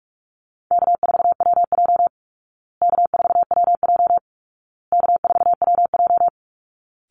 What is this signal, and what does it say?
Morse code, heard as a keyed tone, about 44 wpm, 700 Hz. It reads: K4WJ K4WJ K4WJ
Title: Call Signs Repeated 2x - 45wpm